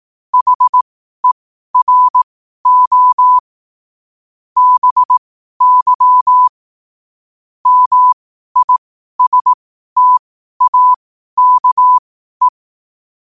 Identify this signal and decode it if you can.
Morse code, heard as a keyed tone, about 18 wpm, spaced as for 12 wpm, 1000 Hz. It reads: HERO BY MISTAKE
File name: qotdfiles / 12wpm